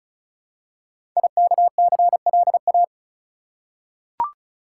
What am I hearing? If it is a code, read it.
Morse code, heard as a keyed tone, about 36 wpm, 700 Hz. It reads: IKCLA
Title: Code Groups - 5 Letter:Numbers - 35wpm